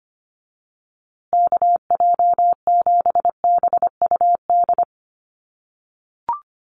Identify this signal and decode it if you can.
Morse code, heard as a keyed tone, about 25 wpm, 700 Hz. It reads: KJ7BUD
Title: USA Call Signs Repeated 2x 25wpm